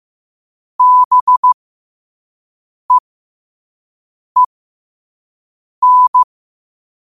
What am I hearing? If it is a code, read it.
Morse code, heard as a keyed tone, about 15 wpm, spaced as for 5 wpm, 1000 Hz. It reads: BEEN
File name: qotdfiles / 5wpm